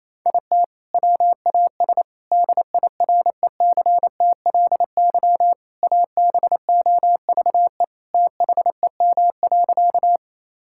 Morse code, heard as a keyed tone, about 28 wpm, 700 Hz. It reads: IT WAS DIRECTLY ABOVE THEM.
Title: Sentences from Top 1000 Words - Review Part 5 - 28wpm